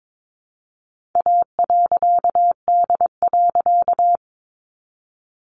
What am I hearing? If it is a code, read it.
Morse code, heard as a keyed tone, about 22 wpm, 700 Hz. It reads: A.D.
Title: A Sound Of Thunder 22wpm